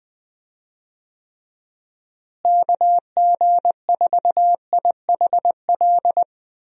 Morse code, heard as a keyed tone, about 21 wpm, 700 Hz. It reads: KG4IHL
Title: USA Call Signs - 6 Characters - Repeated 2x - 20wpm